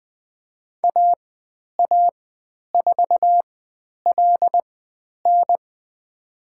Morse code, heard as a keyed tone, about 20 wpm, spaced as for 10 wpm, 700 Hz. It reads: AA4LN
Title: 102 Lesson 17g - Letter u - Call Signs 20wpm